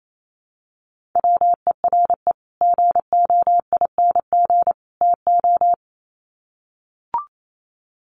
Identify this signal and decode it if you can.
Morse code, heard as a keyed tone, about 28 wpm, 700 Hz. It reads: WERE GOING TO
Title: Sets of 3 Words 28wpm